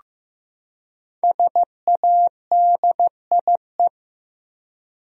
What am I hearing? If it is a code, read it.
Morse code, heard as a keyed tone, about 15 wpm, 700 Hz. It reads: SADIE
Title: USA Common Names - 15wpm